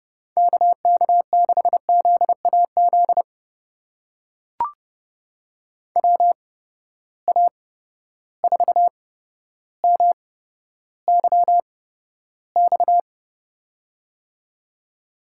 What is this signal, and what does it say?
Morse code, heard as a keyed tone, about 28 wpm, spaced as for 13 wpm, 700 Hz. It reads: KK6ZAZ WA4MYX
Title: USA Call Signs - 6 Characters - 8x Character Spacing - 30wpm